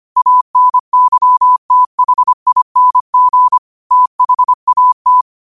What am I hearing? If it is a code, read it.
Morse code, heard as a keyed tone, about 25 wpm, 1000 Hz. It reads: ANYTHING THAT
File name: qotdfiles / 25wpm